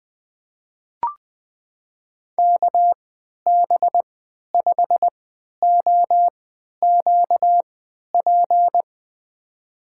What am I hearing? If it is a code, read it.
Morse code, heard as a keyed tone, about 20 wpm, spaced as for 11 wpm, 700 Hz. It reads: KB5OQP
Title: USA Call Signs - 3x Character Spacing - 20wpm